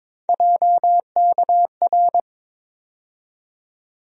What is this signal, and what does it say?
Morse code, heard as a keyed tone, about 22 wpm, 700 Hz. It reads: JKR
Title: USA Call Sign Postfixes - 3 Letters 22wpm